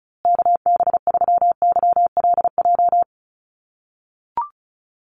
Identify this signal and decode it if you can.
Morse code, heard as a keyed tone, about 34 wpm, 700 Hz. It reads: KB3YLJ
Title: USA Call Signs Repeated 3x 35wpm